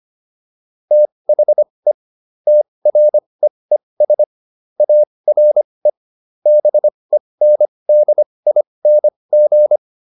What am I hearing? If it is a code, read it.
Morse code, heard as a keyed tone, about 25 wpm, spaced as for 20 wpm, 600 Hz. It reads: THE TREES ARE BENDING